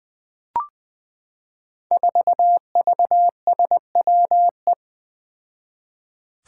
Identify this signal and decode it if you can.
Morse code, heard as a keyed tone, about 20 wpm, 700 Hz. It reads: 4VSWE
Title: Five Challenging Letter-Number Combinations 20wpm